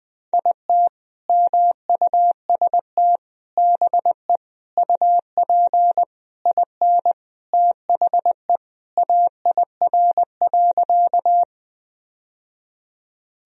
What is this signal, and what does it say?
Morse code, heard as a keyed tone, about 20 wpm, 700 Hz. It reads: IT MUST BE UP IN THE AIR.
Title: Sentences from Top 1000 Words - Review Part 1 - 20wpm